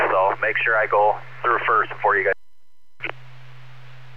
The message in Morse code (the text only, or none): none